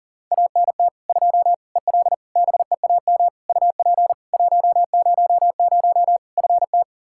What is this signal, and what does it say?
Morse code, heard as a keyed tone, about 37 wpm, 700 Hz. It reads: ANT 2 EL BEAM UP 100 FT
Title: K1USN SST QSO Practice 40wpm